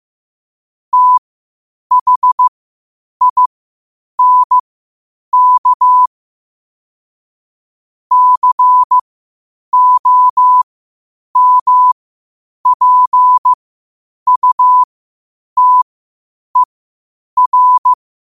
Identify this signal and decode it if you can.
Morse code, heard as a keyed tone, about 15 wpm, spaced as for 8 wpm, 1000 Hz. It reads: THINK COMPUTER
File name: qotdfiles / 8wpm